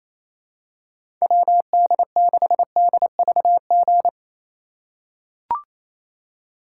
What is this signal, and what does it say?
Morse code, heard as a keyed tone, about 28 wpm, 700 Hz. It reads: WD6DVG
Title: USA Call Signs - 3x Character Spacing - 28wpm